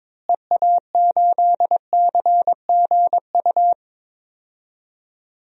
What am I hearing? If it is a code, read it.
Morse code, heard as a keyed tone, about 22 wpm, 700 Hz. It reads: EA8CGU
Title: DXCC-100 Call Signs - 6 Characters - Repeated 2x - 22wpm